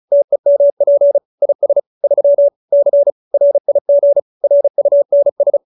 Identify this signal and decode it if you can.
Morse code, heard as a keyed tone, about 35 wpm, 570 Hz. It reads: TEMP IS 3 C RIG RUNS
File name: QSO310 35